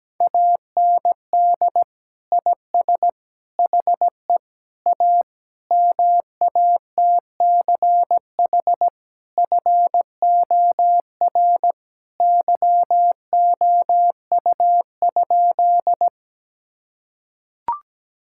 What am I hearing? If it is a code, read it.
Morse code, heard as a keyed tone, about 17 wpm, 700 Hz. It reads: AND IS HE A MATCH FOR YOU?
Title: Sentences from Top 1600 Words 17wpm